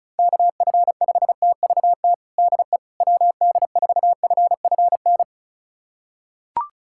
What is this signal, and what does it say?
Morse code, heard as a keyed tone, about 36 wpm, 690 Hz. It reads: KF5TVT DE WD4FFN
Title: USA Callsign DE Callsign - 6 Characters - 35wpm